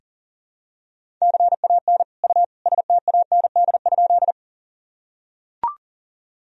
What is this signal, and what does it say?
Morse code, heard as a keyed tone, about 40 wpm, 700 Hz. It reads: CAN U STAND?
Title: Sentences from Top 300 Words and Abbreviations 40wpm